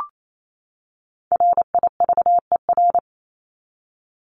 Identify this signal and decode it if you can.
Morse code, heard as a keyed tone, about 28 wpm, 700 Hz. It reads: RIVER
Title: Top 500 Words 28wpm